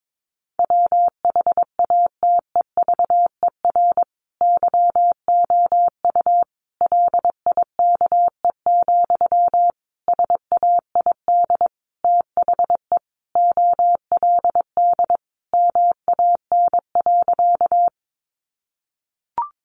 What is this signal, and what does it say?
Morse code, heard as a keyed tone, about 22 wpm, 700 Hz. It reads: WHATEVER YOU LIKE, SAID THE OLD MAN.
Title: Sentences from Top 1100 Words 22wpm